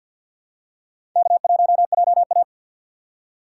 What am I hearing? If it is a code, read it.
Morse code, heard as a keyed tone, about 52 wpm, 700 Hz. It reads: K1JA
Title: USA Call Signs - 3x Character Spacing - 50wpm